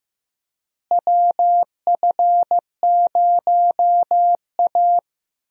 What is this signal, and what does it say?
Morse code, heard as a keyed tone, about 15 wpm, 700 Hz. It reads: WF0A
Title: USA Call Signs - 4 Characters - Repeated 2x - 15wpm